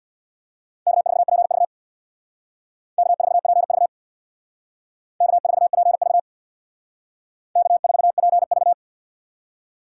Morse code, heard as a keyed tone, about 64 wpm, 700 Hz. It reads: K4PV K4PV K4PV K4PV